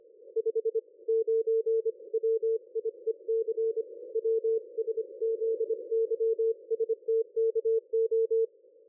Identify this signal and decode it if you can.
Morse code, heard as a keyed tone, about 25 wpm, 450 Hz. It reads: 5 9 WIEC WSZYSTKO
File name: dlazaawansowanych / nagraniazpasma / ExpertSDR2 line out 23.01.2020 14.04.01